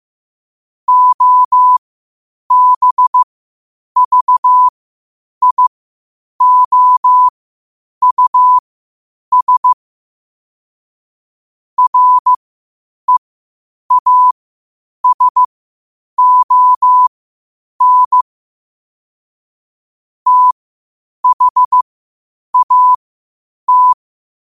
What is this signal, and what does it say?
Morse code, heard as a keyed tone, about 15 wpm, spaced as for 8 wpm, 1000 Hz. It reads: OBVIOUS REASON THAT
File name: qotdfiles / 8wpm